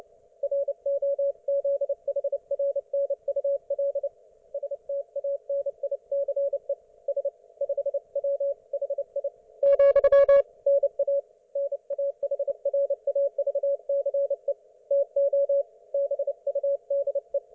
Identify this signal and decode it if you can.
Morse code, heard as a keyed tone, about 29 wpm, 560 Hz. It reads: ROZHRNUL STANICE S 5W HI , NA NAHRAVCE TO BUDE